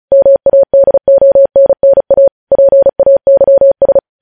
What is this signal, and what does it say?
Morse code, heard as a keyed tone, about 36 wpm, 550 Hz. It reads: MADONNA PAYS